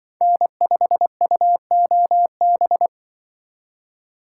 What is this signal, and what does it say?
Morse code, heard as a keyed tone, about 25 wpm, 700 Hz. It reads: N5UOB
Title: Speed Racing - USA 1x3 Call Signs 20wpm